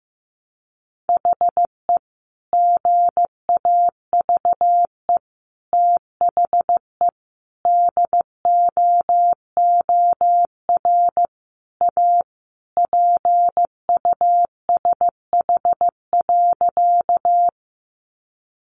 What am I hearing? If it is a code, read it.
Morse code, heard as a keyed tone, about 15 wpm, 700 Hz. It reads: HE GAVE THE DOOR A PUSH.
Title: Sentences from Top 1300 Words 15wpm